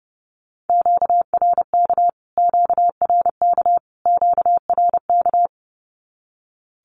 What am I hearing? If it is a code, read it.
Morse code, heard as a keyed tone, about 30 wpm, 700 Hz. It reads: QRK QRK QRK
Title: Q-Codes Repeated 3x 30wpm